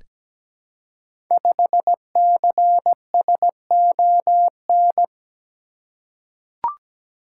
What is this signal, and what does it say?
Morse code, heard as a keyed tone, about 17 wpm, 700 Hz. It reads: 5CSON
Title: Code Groups - 5 Letter:Numbers - 17wpm